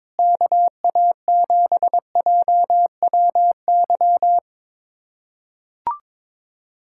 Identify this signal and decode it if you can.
Morse code, heard as a keyed tone, about 22 wpm, 700 Hz. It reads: KA7JWY
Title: USA Callsign DE Callsign - 6 Characters - 22wpm